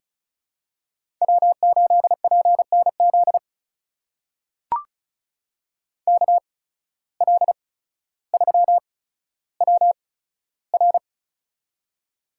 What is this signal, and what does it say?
Morse code, heard as a keyed tone, about 36 wpm, spaced as for 15 wpm, 700 Hz. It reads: W8PNZ KL3WR
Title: USA Call Signs - 5 Characters - 8x Character Spacing - 35wpm